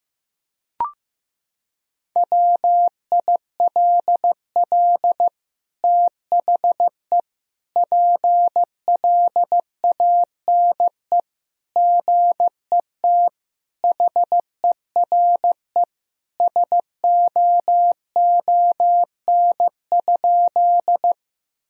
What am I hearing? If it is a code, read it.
Morse code, heard as a keyed tone, about 15 wpm, 700 Hz. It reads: WILL THE PLANE GET HERE SOON?
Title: Sentences from Top 1000 Words - Review Part 3 - 15wpm